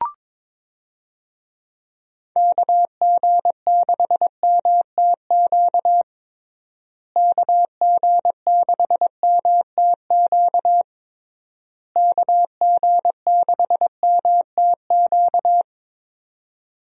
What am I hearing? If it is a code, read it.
Morse code, heard as a keyed tone, about 22 wpm, 700 Hz. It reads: KG6MTQ KG6MTQ KG6MTQ
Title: USA Call Signs - 6 Characters - Repeated 3x - 22wpm